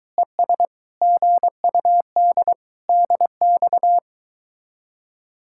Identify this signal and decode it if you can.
Morse code, heard as a keyed tone, about 24 wpm, 700 Hz. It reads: ES GUD DX